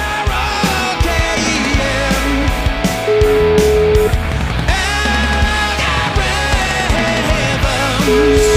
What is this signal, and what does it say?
Ringback tone (Europe, 425 Hz), with noise.